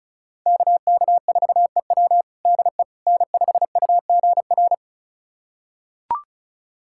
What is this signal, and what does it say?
Morse code, heard as a keyed tone, about 36 wpm, 700 Hz. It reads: KK4EW DE N5UGR
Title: USA Callsign DE Callsign - 5 Characters - 35wpm